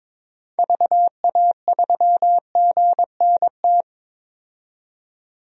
Morse code, heard as a keyed tone, about 22 wpm, 700 Hz. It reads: VA3GNT
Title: DXCC-100 Call Signs - 6 Characters - 22wpm